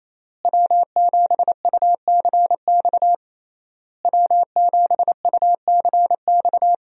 Morse code, heard as a keyed tone, about 28 wpm, 700 Hz. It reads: W7UCX W7UCX
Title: USA Call Signs - 5 Characters - Repeated 3x - 28wpm